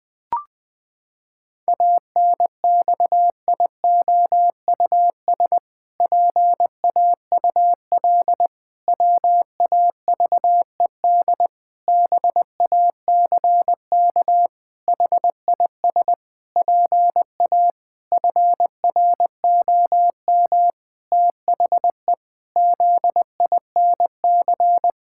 Morse code, heard as a keyed tone, about 20 wpm, 700 Hz. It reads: ANXIOUS PAUL WAVED BACK HIS PA FROM THE ZINC
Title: Pangram Challenge v2 - 20wpm